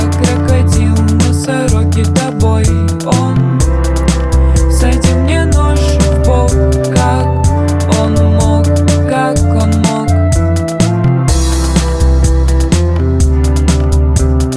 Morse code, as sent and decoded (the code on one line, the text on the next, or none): none